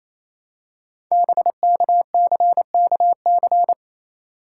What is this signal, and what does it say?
Morse code, heard as a keyed tone, about 28 wpm, 700 Hz. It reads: BKCKC
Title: Code Groups - 5 Letters - 28wpm